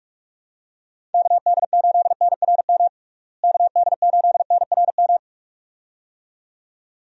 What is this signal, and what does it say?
Morse code, heard as a keyed tone, about 47 wpm, 700 Hz. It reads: KD8NRM KD8NRM
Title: USA Call Signs - 6 Characters - Repeated 2x - 45wpm